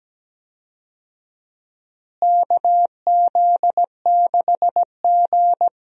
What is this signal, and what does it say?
Morse code, heard as a keyed tone, about 17 wpm, 700 Hz. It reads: KZ6G
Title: USA Call Signs - 4 Characters - Repeated 2x - 17wpm